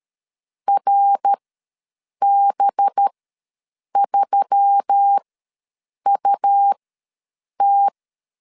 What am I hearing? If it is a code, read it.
Morse code, heard as a keyed tone, about 13 wpm, spaced as for 7 wpm, 800 Hz. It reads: RB3UT
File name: LEZIONI / 5 LBVC / 5 LBV ws 12wpm 3015